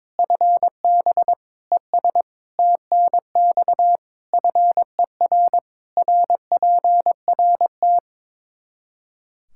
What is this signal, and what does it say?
Morse code, heard as a keyed tone, about 22 wpm, 700 Hz. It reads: FB ES TNX FER RPRT
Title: Practice QSO - Exchange 2 - Info - 22wpm